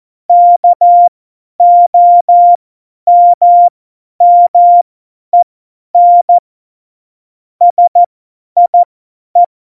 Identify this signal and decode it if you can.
Morse code, heard as a keyed tone, about 14 wpm, spaced as for 10 wpm, 700 Hz. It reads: KOMMEN SIE
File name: Morsekurs / Witze / Witz 02. April 2019, 50 F70